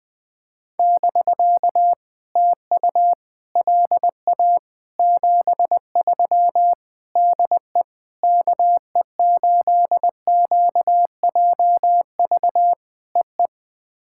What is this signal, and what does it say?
Morse code, heard as a keyed tone, about 20 wpm, 700 Hz. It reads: <BK> TU LA 73 DE KE8QJV EE
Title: POTA QSO Practice - 20wpm